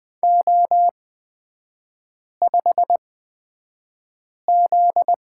Morse code, heard as a keyed tone, about 20 wpm, spaced as for 5 wpm, 700 Hz. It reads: O5Z
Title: 250 Lesson 40e - BK - Three Characters 20wpm